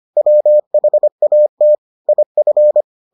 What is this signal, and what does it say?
Morse code, heard as a keyed tone, about 25 wpm, 600 Hz. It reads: WHAT IF